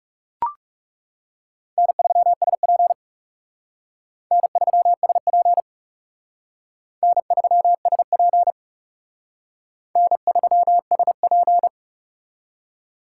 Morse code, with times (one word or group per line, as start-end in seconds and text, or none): 1.8-2.9 N3SP
4.3-5.6 N3SP
7.0-8.5 N3SP
10.0-11.7 N3SP